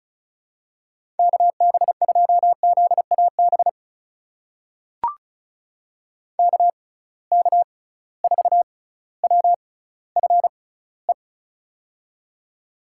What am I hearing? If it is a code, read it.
Morse code, heard as a keyed tone, about 34 wpm, spaced as for 18 wpm, 700 Hz. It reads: KB2ZAB KK4WFE
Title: USA Call Signs - 6 Characters - 6x Character Spacing - 35wpm